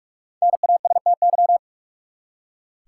Morse code, heard as a keyed tone, about 46 wpm, 700 Hz. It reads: NASTY
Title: Five Letter Words - 45wpm